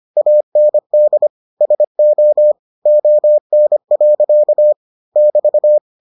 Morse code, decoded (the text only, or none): AND SO ON. <BT>